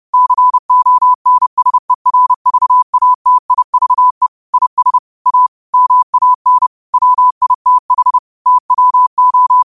Morse code, heard as a keyed tone, about 30 wpm, 1000 Hz. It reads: CONSERVATIVE IS A MAN WITH TWO